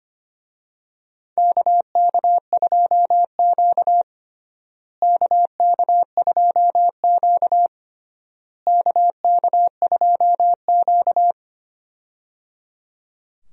Morse code, heard as a keyed tone, about 25 wpm, 700 Hz. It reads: KK2Q KK2Q KK2Q
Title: USA Call Signs - 4 Characters - Repeated 3x - 25wpm